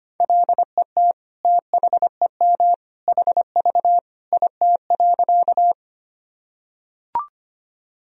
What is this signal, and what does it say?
Morse code, heard as a keyed tone, about 25 wpm, 700 Hz. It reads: LET THEM HV IT.
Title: Sentences from Top 300 Words and Abbreviations 25wpm